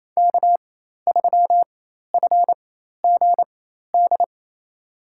Morse code, heard as a keyed tone, about 28 wpm, spaced as for 13 wpm, 700 Hz. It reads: K3FGD
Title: USA Call Signs - 5 Characters - 4x Character Spacing - 28wpm